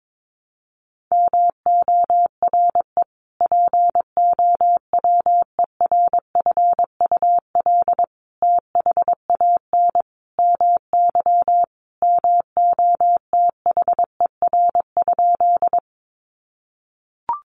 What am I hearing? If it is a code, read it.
Morse code, heard as a keyed tone, about 22 wpm, 700 Hz. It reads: MORE POWERFUL THAN MY MOTHER?
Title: Sentences from Top 1300 Words 22wpm